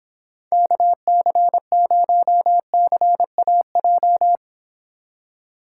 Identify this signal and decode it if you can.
Morse code, heard as a keyed tone, about 26 wpm, 700 Hz. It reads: KC0CAJ